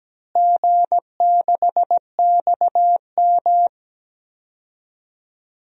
Morse code, heard as a keyed tone, about 17 wpm, 700 Hz. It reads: G6XM
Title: DXCC-100 Call Signs - 4 Characters - Repeated 3x - 17wpm